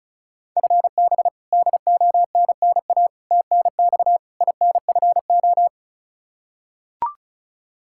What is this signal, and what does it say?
Morse code, heard as a keyed tone, about 34 wpm, 700 Hz. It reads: FB DONNA TNX INFO
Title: Practice QSO - Exchange 3 - More Info - 35wpm